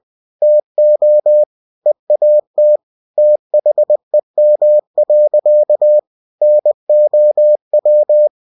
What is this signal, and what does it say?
Morse code, heard as a keyed tone, about 20 wpm, 600 Hz. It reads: TO EAT THEM. NOW
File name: cw-books / winnie-the-pooh / winnie-20wpm-page-0078